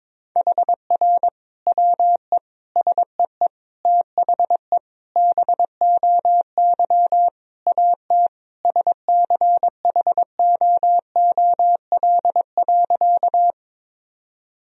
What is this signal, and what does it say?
Morse code, heard as a keyed tone, about 22 wpm, 700 Hz. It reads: HR WE SEE THE BOY AT SCHOOL.
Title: Sentences from Top 300 Words and Abbreviations 22wpm